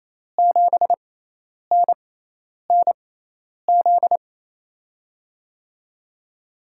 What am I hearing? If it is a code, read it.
Morse code, heard as a keyed tone, about 28 wpm, spaced as for 10 wpm, 700 Hz. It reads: 7NNZ
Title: USA Call Signs - 6x Character Spacing - 28wpm